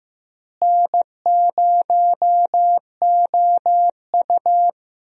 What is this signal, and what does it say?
Morse code, heard as a keyed tone, about 15 wpm, 700 Hz. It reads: N0OU
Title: USA Call Signs - 4 Characters - Repeated 3x - 15wpm